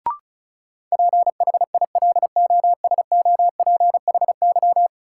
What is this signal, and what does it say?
Morse code, heard as a keyed tone, about 34 wpm, 700 Hz. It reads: PHILOSOPHY
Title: Sentences from Top 900 Words - New Word Drill - 35wpm